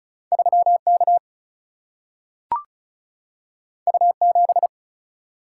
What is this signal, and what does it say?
Morse code, heard as a keyed tone, about 36 wpm, 700 Hz. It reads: 3 K U 7
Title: Single-Letter Single-Number 35wpm